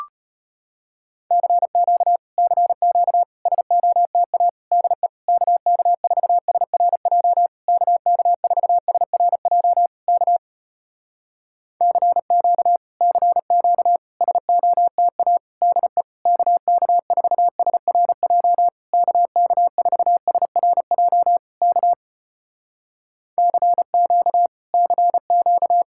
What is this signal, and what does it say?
Morse code, heard as a keyed tone, about 35 wpm, 700 Hz. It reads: CQ CQ SOTA DE KK4SRJ KK4SRJ K CQ CQ SOTA DE KK4SRJ KK4SRJ K CQ CQ